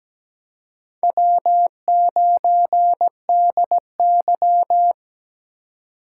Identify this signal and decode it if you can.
Morse code, heard as a keyed tone, about 17 wpm, 700 Hz. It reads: W9DY
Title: K1USN SST QSO Practice - 17wpm